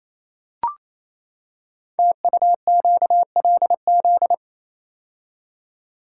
Morse code, heard as a keyed tone, about 29 wpm, 700 Hz. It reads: TUQLZ
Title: Code Groups - 5 Letters - 28wpm